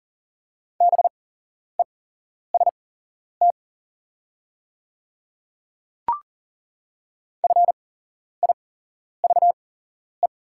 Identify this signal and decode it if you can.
Morse code, heard as a keyed tone, about 40 wpm, spaced as for 11 wpm, 690 Hz. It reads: BEST FIVE
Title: 8x Character Spacing - Top 500 Words - 40wpm